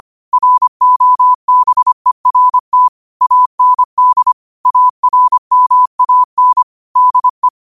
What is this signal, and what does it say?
Morse code, heard as a keyed tone, about 25 wpm, 1000 Hz. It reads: ROBERT AND ARMAN DE
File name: qotdfiles / 25wpm